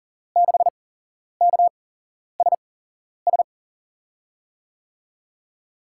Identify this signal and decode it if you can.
Morse code, heard as a keyed tone, about 37 wpm, spaced as for 11 wpm, 700 Hz. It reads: 6KSS
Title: USA Call Signs - 6 Characters - 8x Character Spacing - 40wpm